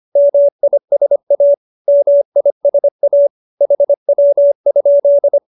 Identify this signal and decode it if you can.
Morse code, heard as a keyed tone, about 25 wpm, 570 Hz. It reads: MISA MISA HW?